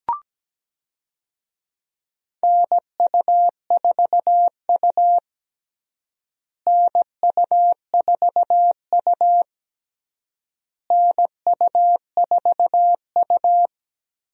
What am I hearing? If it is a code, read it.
Morse code, heard as a keyed tone, about 17 wpm, 700 Hz. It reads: NU4U NU4U NU4U
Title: USA Call Signs - 4 Characters - Repeated 3x - 17wpm